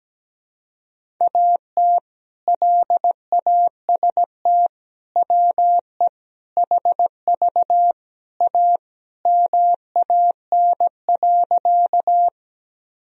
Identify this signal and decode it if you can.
Morse code, heard as a keyed tone, about 17 wpm, 700 Hz. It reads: AT LAST WE HV A MAN.
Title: Sentences from Top 300 Words and Abbreviations 17wpm